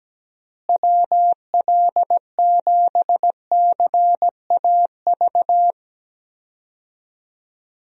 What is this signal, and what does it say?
Morse code, heard as a keyed tone, about 17 wpm, 700 Hz. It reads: WL7CAV
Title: USA Call Signs - 6 Characters - Repeated 3x - 17wpm